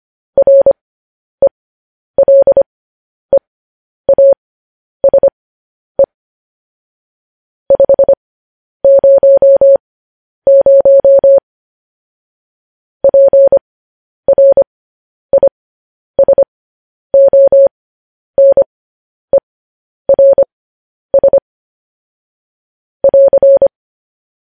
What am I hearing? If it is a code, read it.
Morse code, heard as a keyed tone, about 25 wpm, spaced as for 10 wpm, 550 Hz. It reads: RELEASE 500 PRISONERS <AR>
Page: News Headlines in Morse Code at 10 WPM